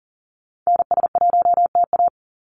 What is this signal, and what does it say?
Morse code, heard as a keyed tone, about 40 wpm, 700 Hz. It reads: NS1TA
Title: NA SOTA Activator Callsigns 40wpm